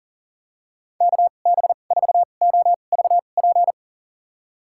Morse code, heard as a keyed tone, about 37 wpm, spaced as for 28 wpm, 700 Hz. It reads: KB4OVP
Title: USA Call Signs - 2x Character Spacing - 40wpm